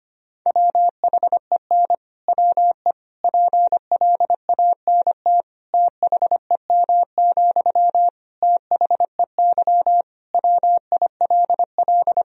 Morse code, heard as a keyed tone, about 25 wpm, 700 Hz. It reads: WHEN WE PLANT THEM, THEY WILL
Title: Sentences from Top 1000 Words - Review Part 1 - 25wpm